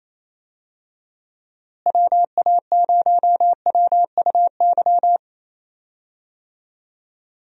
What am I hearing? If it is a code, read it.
Morse code, heard as a keyed tone, about 27 wpm, 700 Hz. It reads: WA0WUY
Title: USA Call Signs - 6 Characters - Repeated 3x - 28wpm